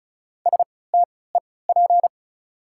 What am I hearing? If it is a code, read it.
Morse code, heard as a keyed tone, about 36 wpm, spaced as for 20 wpm, 690 Hz. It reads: STEP
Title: 3x Character Spacing - Top 500 Words - 35wpm